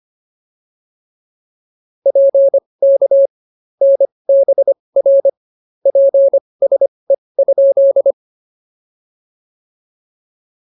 Morse code, heard as a keyed tone, about 25 wpm, spaced as for 20 wpm, 590 Hz. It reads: PK NBR PSE?
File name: POTA207 20